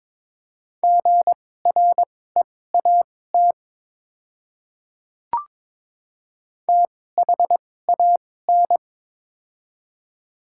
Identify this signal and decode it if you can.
Morse code, heard as a keyed tone, about 21 wpm, spaced as for 16 wpm, 700 Hz. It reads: GREAT THAN
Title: 2x Character Spacing - Top 300 Words - 22wpm